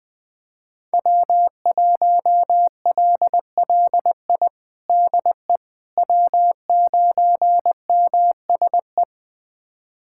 Morse code, heard as a keyed tone, about 19 wpm, 700 Hz. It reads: W1LLI DE W9MSE
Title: USA Callsign DE Callsign - 5 Characters - 20wpm